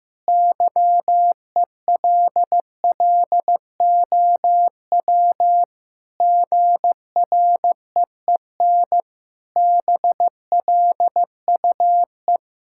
Morse code, heard as a keyed tone, about 14 wpm, 700 Hz. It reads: YELLOW GREEN BLUE
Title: Pangram Challenge v2 - 15wpm